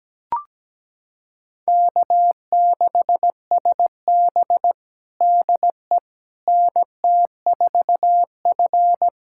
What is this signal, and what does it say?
Morse code, heard as a keyed tone, about 17 wpm, 700 Hz. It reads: K6SB DE NT4F
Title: USA Callsign DE Callsign - 4 Characters - 17wpm